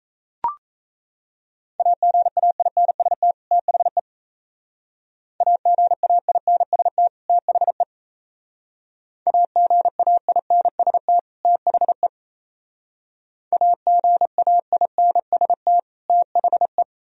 Morse code, heard as a keyed tone, about 33 wpm, 690 Hz. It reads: AGAINST THE AGAINST THE AGAINST THE AGAINST THE
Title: Speed-Racing - Sets of 2 Words 28wpm